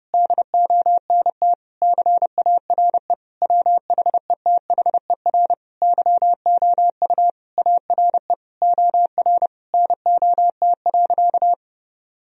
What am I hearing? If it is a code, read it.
Morse code, heard as a keyed tone, about 32 wpm, 700 Hz. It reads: DONT CARE WHETHER YOU ARE OR NOT.
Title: Sentences from Top 1000 Words - Review Part 3 - 30wpm